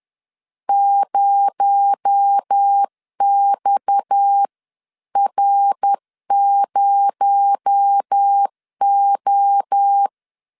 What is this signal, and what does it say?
Morse code, heard as a keyed tone, about 10 wpm, 800 Hz. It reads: 0XR0O
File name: CW / DET / 1 9 1500C S A 10wpm 4511